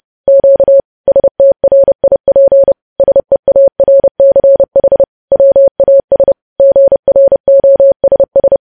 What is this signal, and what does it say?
Morse code, heard as a keyed tone, about 30 wpm, 550 Hz. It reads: Q STRIP SEARCH WAS GROSS
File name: podcast / News headlines-1x-30x30